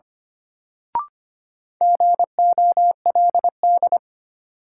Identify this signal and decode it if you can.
Morse code, heard as a keyed tone, about 25 wpm, 700 Hz. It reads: GOLD
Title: Sentences from Top 1000 Words - Review Part 3 - New Word Drill - 25wpm